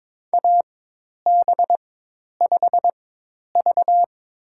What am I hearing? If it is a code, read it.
Morse code, heard as a keyed tone, about 22 wpm, spaced as for 10 wpm, 700 Hz. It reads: AB5V
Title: USA Call Signs - 4 Characters - 4x Character Spacing - 22wpm